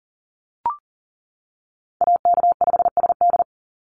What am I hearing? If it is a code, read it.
Morse code, heard as a keyed tone, about 40 wpm, 700 Hz. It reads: AK5SD
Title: NA SOTA Chaser Callsigns 40wpm